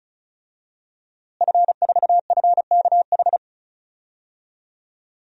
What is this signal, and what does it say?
Morse code, heard as a keyed tone, about 36 wpm, 700 Hz. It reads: F4FKH
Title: DXCC-100 Call Signs - 5 Characters - Repeated 3x - 35wpm